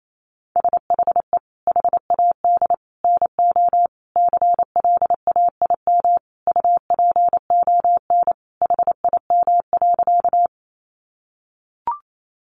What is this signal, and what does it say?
Morse code, heard as a keyed tone, about 28 wpm, 700 Hz. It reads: SHE HAD NO CLAIM UPON HIM.
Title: Sentences from Top 1500 Words 28wpm